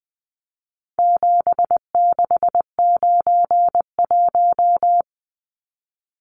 Morse code, heard as a keyed tone, about 20 wpm, 700 Hz. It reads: 7691
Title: Numbers - 4-Digit 20wpm